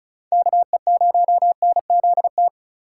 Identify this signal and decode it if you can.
Morse code, heard as a keyed tone, about 36 wpm, 700 Hz. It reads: KE0NZT
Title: USA Call Signs - 6 Characters - 2x Character Spacing - 35wpm